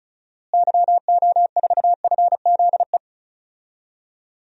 Morse code, heard as a keyed tone, about 34 wpm, 700 Hz. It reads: YO4FZE
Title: DXCC-100 Call Signs - 6 Characters - Repeated 2x - 35wpm